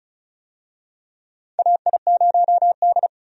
Morse code, heard as a keyed tone, about 36 wpm, 690 Hz. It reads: AI0D
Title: USA Call Signs - 4 Characters - 35wpm